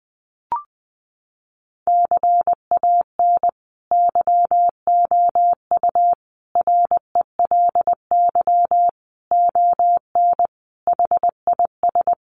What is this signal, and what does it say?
Morse code, heard as a keyed tone, about 20 wpm, 700 Hz. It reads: CAN YOU RELY ON HIS
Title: Sentences from Top 1200 Words 20wpm